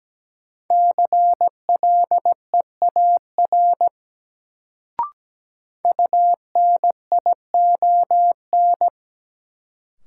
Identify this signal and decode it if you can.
Morse code, heard as a keyed tone, about 17 wpm, 700 Hz. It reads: CLEAR UNION
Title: Sets of 5 Words Encore - New Word Drill - 17wpm